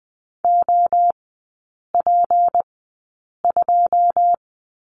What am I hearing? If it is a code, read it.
Morse code, heard as a keyed tone, about 20 wpm, spaced as for 8 wpm, 690 Hz. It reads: OP2
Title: 159 Lesson 26e - Letter g - Three Characters 20wpm